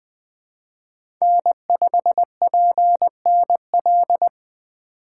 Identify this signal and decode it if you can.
Morse code, heard as a keyed tone, about 20 wpm, 700 Hz. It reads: N5PNL
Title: USA Call Signs - 5 Characters - 2x Character Spacing - 20wpm